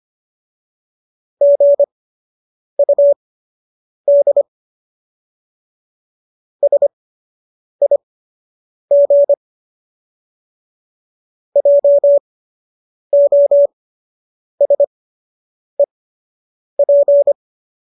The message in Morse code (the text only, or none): GUD SIG JOSEP